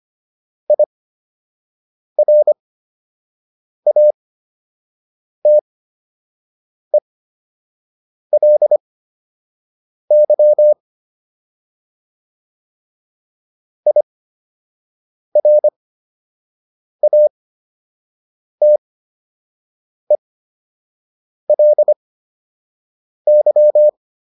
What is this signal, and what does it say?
Morse code, heard as a keyed tone, about 25 wpm, spaced as for 6 wpm, 600 Hz. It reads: IRATELY IRATELY